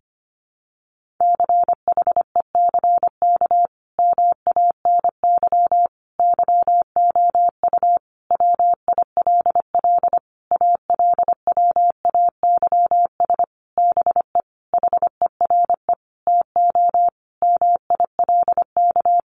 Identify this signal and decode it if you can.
Morse code, heard as a keyed tone, about 25 wpm, 700 Hz. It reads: CHECK MANY YOU WILL ALWAYS BE HERE TO MILK
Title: Sentences from Top 1500 Words 25wpm